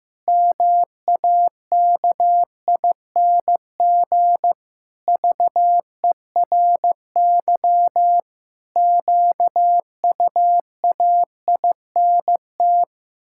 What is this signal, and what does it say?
Morse code, heard as a keyed tone, about 15 wpm, 700 Hz. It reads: MAKING VERY QUAINT
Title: Pangram Challenge v2 - 15wpm